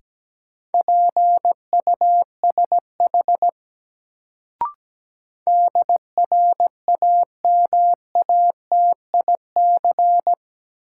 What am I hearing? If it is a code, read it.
Morse code, heard as a keyed tone, about 17 wpm, 700 Hz. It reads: PUSH DRAMATIC
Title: Sentences from Top 1300 Words - New Word Drill - 17wpm